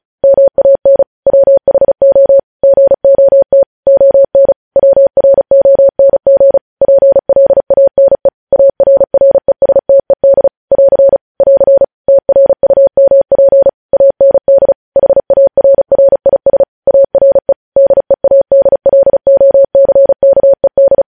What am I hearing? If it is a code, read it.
Morse code, heard as a keyed tone, about 36 wpm, 550 Hz. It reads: MAN WHO GOT ON WRONG PLANE ARRESTED <AR> <AR> TRUMP AND HARRIS ARE DEADLOCKED